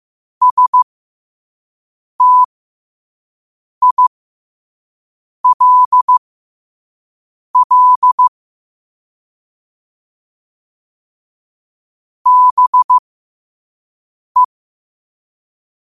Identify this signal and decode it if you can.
Morse code, heard as a keyed tone, about 15 wpm, spaced as for 5 wpm, 1000 Hz. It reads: STILL BE
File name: qotdfiles / 5wpm